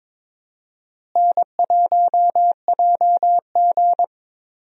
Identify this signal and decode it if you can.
Morse code, heard as a keyed tone, about 22 wpm, 700 Hz. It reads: N1JG
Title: USA Call Signs - 4 Characters - Repeated 3x - 22wpm